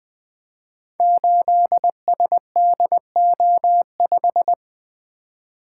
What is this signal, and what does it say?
Morse code, heard as a keyed tone, about 20 wpm, 700 Hz. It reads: 8SDO5
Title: Code Groups - 5 Letter:Numbers - 20wpm